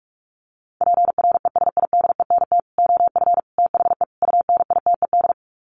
Morse code, heard as a keyed tone, about 44 wpm, 700 Hz. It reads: PRESIDENT OF THE UNITED
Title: Sets of 4 Words 45wpm